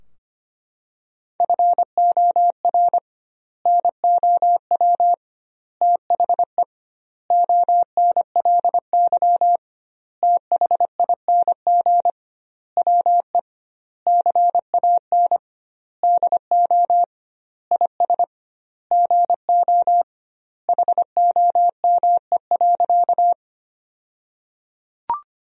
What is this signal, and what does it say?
Morse code, heard as a keyed tone, about 25 wpm, 700 Hz. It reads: FOR NOW THE ONLY THING WE CAN DO IS GO HOME.
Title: 2x Word-Spacing - Sentences from Top 200 Words - 25wpm